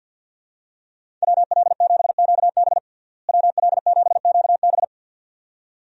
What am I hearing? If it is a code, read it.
Morse code, heard as a keyed tone, about 50 wpm, 700 Hz. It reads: WL7QB WL7QB
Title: USA Call Signs - 5 Characters - Repeated 2x - 50wpm